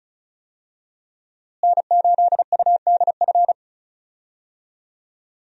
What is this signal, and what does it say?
Morse code, heard as a keyed tone, about 34 wpm, 700 Hz. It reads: N8UDF
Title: USA Call Signs - 5 Characters - Repeated 2x - 35wpm